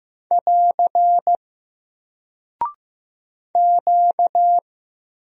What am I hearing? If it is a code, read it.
Morse code, heard as a keyed tone, about 15 wpm, 700 Hz. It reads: <AR> Q
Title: Single Character (Letter:Number:Punctuation:Prosign) Alt 15wpm